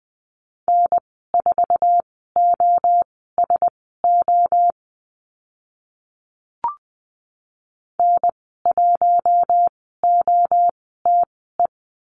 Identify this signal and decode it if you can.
Morse code, heard as a keyed tone, about 20 wpm, spaced as for 15 wpm, 700 Hz. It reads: N4OSO N1OTE
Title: 069 Lesson 12g - Letter d - Call Signs 20wpm